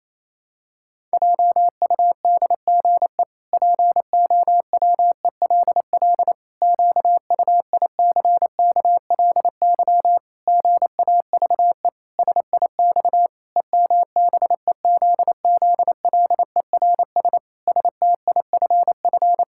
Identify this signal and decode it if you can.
Morse code, heard as a keyed tone, about 28 wpm, 700 Hz. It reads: JUDGE POWELL QUICKLY GAVE SIX EMBEZZLERS STIFF
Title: Pangram Challenge v2 - 28wpm